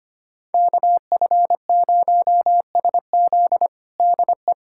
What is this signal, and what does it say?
Morse code, heard as a keyed tone, about 25 wpm, 700 Hz. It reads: KF0SZ DE
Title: USA Callsign DE Callsign - 5 Characters - 25wpm